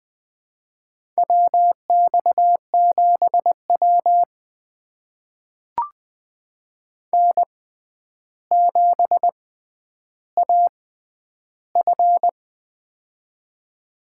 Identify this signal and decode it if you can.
Morse code, heard as a keyed tone, about 20 wpm, spaced as for 10 wpm, 700 Hz. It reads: WX7W N7AF
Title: USA Call Signs - 4 Characters - 6x Character Spacing - 20wpm